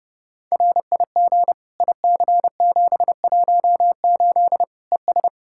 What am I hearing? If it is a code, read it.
Morse code, heard as a keyed tone, about 30 wpm, 700 Hz. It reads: RIG IC718 ES
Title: Practice QSO - Exchange 2 - Info - 30wpm